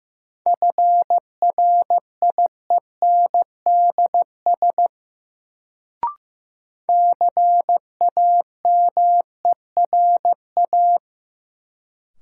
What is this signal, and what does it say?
Morse code, heard as a keyed tone, about 15 wpm, 700 Hz. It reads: FRIENDS CAMERA
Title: Sentences from Top 1000 Words - Review Part 5 - New Word Drill - 15wpm